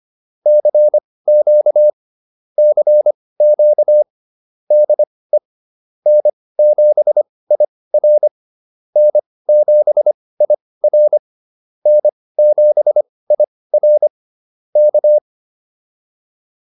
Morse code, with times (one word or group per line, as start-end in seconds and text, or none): 0.5-1.9 CQ
2.6-4.0 CQ
4.7-5.4 DE
6.1-8.3 N7IR
9.0-11.2 N7IR
11.9-14.1 N7IR
14.8-15.2 K